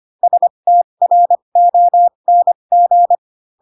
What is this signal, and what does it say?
Morse code, heard as a keyed tone, about 25 wpm, spaced as for 22 wpm, 700 Hz. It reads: STRONG